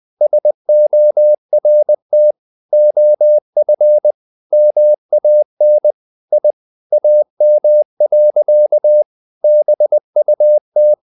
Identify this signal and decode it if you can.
Morse code, heard as a keyed tone, about 20 wpm, 600 Hz. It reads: SORT OF MAN I AM. BUT
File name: the old man and the sea / the old man and the sea part 1 20wpm